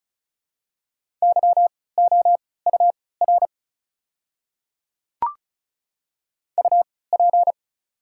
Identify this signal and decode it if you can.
Morse code, heard as a keyed tone, about 36 wpm, spaced as for 20 wpm, 690 Hz. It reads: YOUR UP
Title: 3x Character Spacing - Top 100 Words - 35wpm